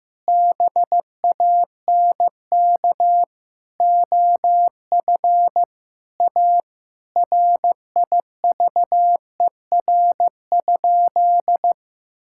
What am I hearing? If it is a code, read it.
Morse code, heard as a keyed tone, about 15 wpm, 700 Hz. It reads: BANK OF A RIVER?
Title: Sentences from Top 1000 Words - Review Part 6 - 15wpm